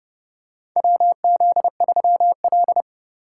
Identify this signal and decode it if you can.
Morse code, heard as a keyed tone, about 30 wpm, 700 Hz. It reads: WZ3L
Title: USA Call Signs - 3x Character Spacing - 30wpm